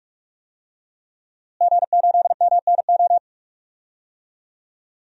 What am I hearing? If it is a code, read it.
Morse code, heard as a keyed tone, about 46 wpm, 700 Hz. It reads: G8MNO
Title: DXCC-100 Call Signs - 5 Characters - Repeated 2x - 45wpm